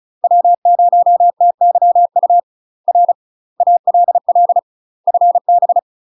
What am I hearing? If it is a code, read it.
Morse code, heard as a keyed tone, about 35 wpm, 700 Hz. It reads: W0TYU R ALL FB
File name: UK MASTER QSOs TRAINING/35WPM